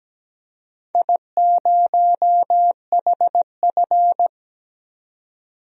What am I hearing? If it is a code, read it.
Morse code, heard as a keyed tone, about 17 wpm, 700 Hz. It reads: I0HF
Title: DXCC-100 Call Signs - 4 Characters - 17wpm